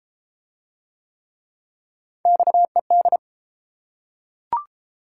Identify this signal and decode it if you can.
Morse code, heard as a keyed tone, about 31 wpm, 700 Hz. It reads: XED